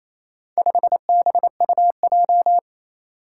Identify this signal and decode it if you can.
Morse code, heard as a keyed tone, about 27 wpm, 700 Hz. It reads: 5BUJ
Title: Four Challenging Letter-Number Combinations 28wpm